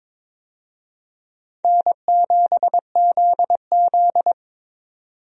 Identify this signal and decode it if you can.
Morse code, heard as a keyed tone, about 22 wpm, 700 Hz. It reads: N7ZZ
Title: USA Call Signs - 4 Characters - 22wpm